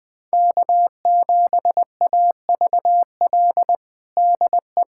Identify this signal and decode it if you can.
Morse code, heard as a keyed tone, about 20 wpm, 700 Hz. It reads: K7AVL DE
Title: USA Callsign DE Callsign - 5 Characters - 20wpm